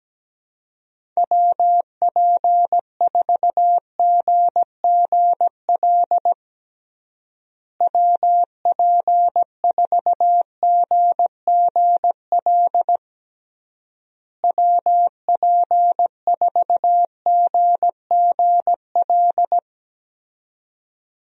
Morse code, heard as a keyed tone, about 17 wpm, 700 Hz. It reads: WP4GGL WP4GGL WP4GGL
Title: USA Call Signs - 6 Characters - Repeated 3x - 17wpm